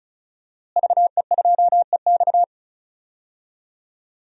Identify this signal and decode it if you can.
Morse code, heard as a keyed tone, about 34 wpm, 700 Hz. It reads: VE2EX
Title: DXCC-100 Call Signs - 5 Characters - Repeated 2x - 35wpm